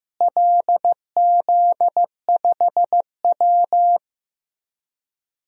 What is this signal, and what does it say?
Morse code, heard as a keyed tone, about 15 wpm, 700 Hz. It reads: LZ5W
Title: DXCC-100 Call Signs - 4 Characters - Repeated 2x - 15wpm